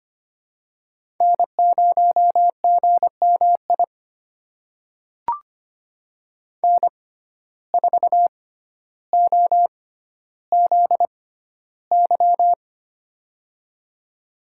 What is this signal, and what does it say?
Morse code, heard as a keyed tone, about 25 wpm, spaced as for 13 wpm, 700 Hz. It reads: N0GMI N4OZY
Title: USA Call Signs - 5 Characters - 6x Character Spacing - 25wpm